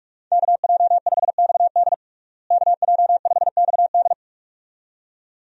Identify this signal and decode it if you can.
Morse code, heard as a keyed tone, about 46 wpm, 700 Hz. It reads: KJ5XD KJ5XD
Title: USA Call Signs - 5 Characters - Repeated 2x - 45wpm